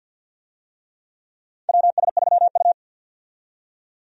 Morse code, heard as a keyed tone, about 51 wpm, 700 Hz. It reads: WS3U
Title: USA Call Signs - 4 Characters - 50wpm